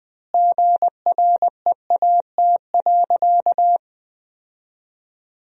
Morse code, heard as a keyed tone, about 20 wpm, 700 Hz. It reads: GREAT.
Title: 3x Word-Spacing - Sentences from Top 200 Words - 20wpm